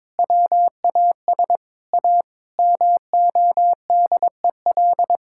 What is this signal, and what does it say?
Morse code, heard as a keyed tone, about 22 wpm, 700 Hz. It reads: WAS A MODEL
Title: Sentences from Top 1000 Words - Review Part 5 - 22wpm